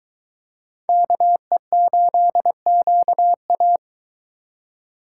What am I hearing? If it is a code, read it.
Morse code, heard as a keyed tone, about 21 wpm, 700 Hz. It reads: KE8QA